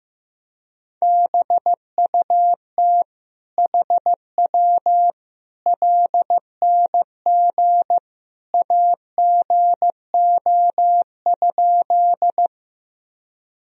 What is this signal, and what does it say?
Morse code, heard as a keyed tone, about 15 wpm, 700 Hz. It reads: BUT HW LNG AGO?
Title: Sentences from Top 500 Words and Abbreviations 15wpm